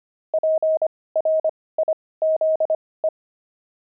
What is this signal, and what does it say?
Morse code, heard as a keyed tone, about 25 wpm, spaced as for 18 wpm, 630 Hz. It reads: PRIZE
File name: pangram-fs-18-cs-25-buckles